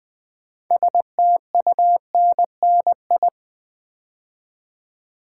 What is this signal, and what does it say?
Morse code, heard as a keyed tone, about 20 wpm, 700 Hz. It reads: STUNNI
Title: Build A Word - 3 Letters At A Time - 8-Letter Words - 20wpm